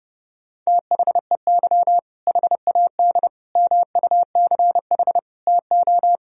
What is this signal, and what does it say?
Morse code, heard as a keyed tone, about 30 wpm, 700 Hz. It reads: THEY HAD MUCH TO
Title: Sentences from Top 400 Words and Abbreviations 30wpm